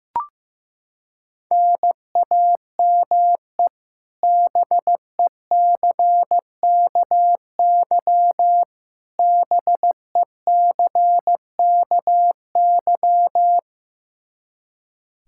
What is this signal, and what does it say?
Morse code, heard as a keyed tone, about 16 wpm, 700 Hz. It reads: NAME BECKY BECKY
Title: K1USN SST QSO Practice 15wpm